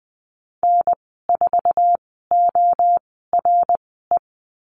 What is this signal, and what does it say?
Morse code, heard as a keyed tone, about 20 wpm, spaced as for 15 wpm, 700 Hz. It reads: N4ORE
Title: 076 Lesson 13g - Letter L - Call Signs 20wpm